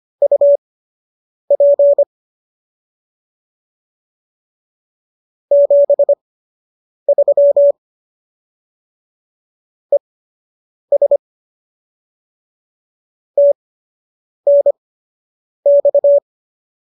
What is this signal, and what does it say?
Morse code, heard as a keyed tone, about 25 wpm, spaced as for 8 wpm, 570 Hz. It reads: UP 73 ES TNX